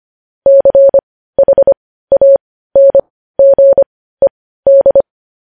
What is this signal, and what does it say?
Morse code, heard as a keyed tone, about 25 wpm, spaced as for 15 wpm, 550 Hz. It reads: CHANGED
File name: podcast / News headlines-1x-25x15